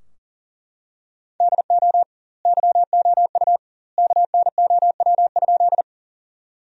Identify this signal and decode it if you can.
Morse code, heard as a keyed tone, about 40 wpm, 700 Hz. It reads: DO YOU KNOW?
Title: 2x Word-Spacing - Sentences from Top 100 Words - 40wpm